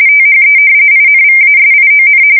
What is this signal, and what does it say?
RTTY, about 45.45 baud. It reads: THE INVASION
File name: fsk170 / battleshipnj